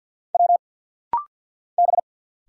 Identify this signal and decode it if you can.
Morse code, heard as a keyed tone, about 50 wpm, 690 Hz. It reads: W B
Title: Single Letters - Rapid-Fire 50wpm